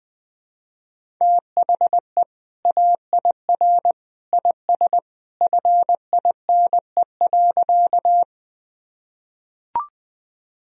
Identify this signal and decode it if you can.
Morse code, heard as a keyed tone, about 20 wpm, 700 Hz. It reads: THE AIR IS FINE.
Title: Sentences from Top 1000 Words - Review Part 1 - 20wpm